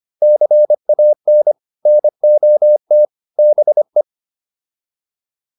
Morse code, heard as a keyed tone, about 25 wpm, 600 Hz. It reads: CAN NOT BE
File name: WD205 25